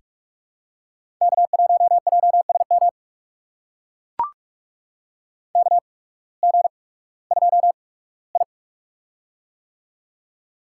Morse code, heard as a keyed tone, about 44 wpm, spaced as for 21 wpm, 700 Hz. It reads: K1JSM KG2I
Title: USA Call Signs - 8x Character Spacing - 45wpm